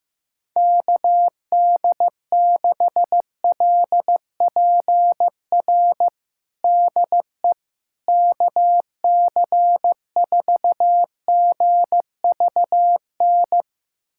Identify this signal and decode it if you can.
Morse code, heard as a keyed tone, about 15 wpm, 700 Hz. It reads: KD6LPR DE KC4GVN
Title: USA Callsign DE Callsign - 6 Characters - 15wpm